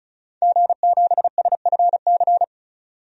Morse code, heard as a keyed tone, about 34 wpm, 700 Hz. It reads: G7SFC
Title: DXCC-100 Call Signs - 5 Characters - 35wpm